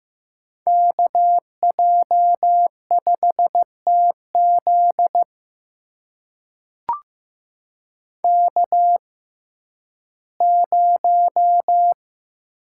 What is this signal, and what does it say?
Morse code, heard as a keyed tone, about 15 wpm, spaced as for 11 wpm, 700 Hz. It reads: KJ5TZ K0
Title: USA Call Signs - 5 Characters - 6x Character Spacing - 15wpm